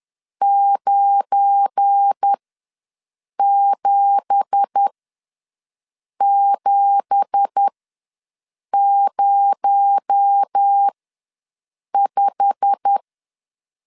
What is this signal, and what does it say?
Morse code, heard as a keyed tone, about 10 wpm, spaced as for 6 wpm, 800 Hz. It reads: 97705
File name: LEZIONI / 9 2867 ws 10wpm 4215